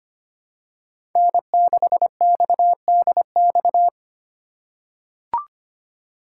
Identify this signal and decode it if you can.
Morse code, heard as a keyed tone, about 25 wpm, 700 Hz. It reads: N6XDX
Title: USA Call Signs - 5 Characters - 3x Character Spacing - 25wpm